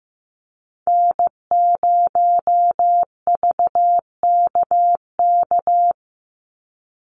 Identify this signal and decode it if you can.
Morse code, heard as a keyed tone, about 15 wpm, 700 Hz. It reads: N0VKK
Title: USA Call Signs 15wpm